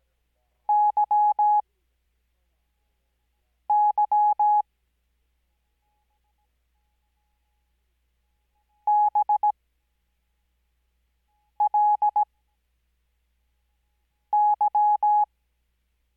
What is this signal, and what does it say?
Morse code, heard as a keyed tone, about 17 wpm, spaced as for 4 wpm, 840 Hz. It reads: YY BLY
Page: Farnsworth Spacing: 15WPM Character speed Spaced at 5WPM.